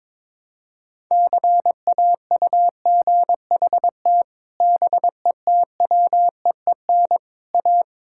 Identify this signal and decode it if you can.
Morse code, heard as a keyed tone, about 21 wpm, 700 Hz. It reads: CAUGHT BETWEEN A
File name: practice / 22wpm / Sets of 8 Words - 22wpm